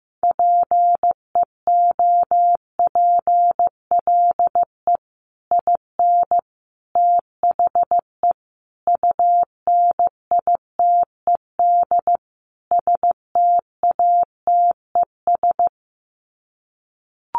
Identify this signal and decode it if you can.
Morse code, heard as a keyed tone, about 14 wpm, 700 Hz. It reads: PEOPLE IN THE UNITED STATES
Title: Sets of 5 Words Encore 15wpm